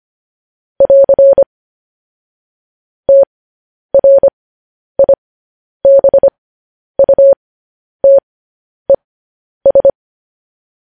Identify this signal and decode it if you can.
Morse code, heard as a keyed tone, about 25 wpm, spaced as for 10 wpm, 550 Hz. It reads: <AR> TRIBUTES
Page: News Headlines in Morse Code at 10 WPM